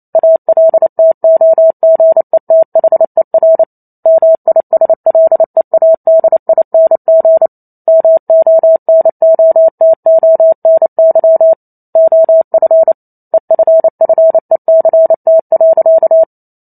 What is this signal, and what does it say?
Morse code, heard as a keyed tone, about 28 wpm, 650 Hz. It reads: ALTOGETHER MISLEADING MONOTONY OF EFFECT.